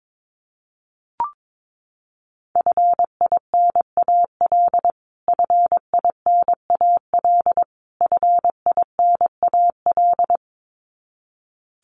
Morse code, heard as a keyed tone, about 22 wpm, 700 Hz. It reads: FINAL FINAL FINAL
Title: Top 500 Words Repeated 22wpm